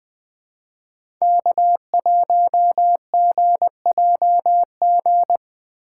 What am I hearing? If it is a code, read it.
Morse code, heard as a keyed tone, about 20 wpm, 700 Hz. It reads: K1GJG
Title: USA Call Signs - 5 Characters - 6x Character Spacing - 20wpm